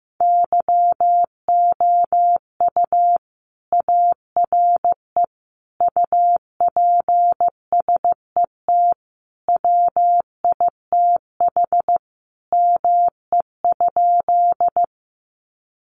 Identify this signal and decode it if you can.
Morse code, heard as a keyed tone, about 15 wpm, 700 Hz. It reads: YOU ARE UPSET WITH ME?
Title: Sentences from Top 800 Words 15wpm